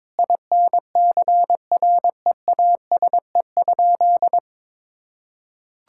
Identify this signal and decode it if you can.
Morse code, heard as a keyed tone, about 22 wpm, 700 Hz. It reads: INCREASE?
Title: Sentences from Top 1000 Words - Review Part 5 - 22wpm